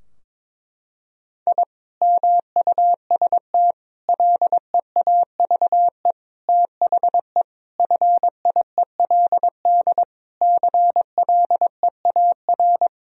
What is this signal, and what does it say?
Morse code, heard as a keyed tone, about 22 wpm, 700 Hz. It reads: I MUST LEAVE THE FIELD CLEAR
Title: Sentences from Top 1000 Words - Review Part 2 - 22wpm